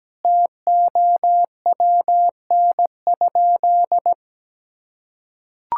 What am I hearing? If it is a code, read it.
Morse code, heard as a keyed tone, about 17 wpm, 700 Hz. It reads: TOWN?
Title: Sentences from Top 1000 Words - Review Part 6 - 17wpm